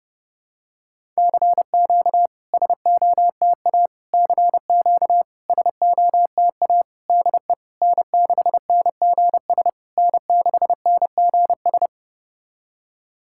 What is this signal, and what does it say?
Morse code, heard as a keyed tone, about 30 wpm, 700 Hz. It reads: CQ SOTA CQ SOTA DE N6NGS N6NGS
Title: SOTA QSO Practice 30wpm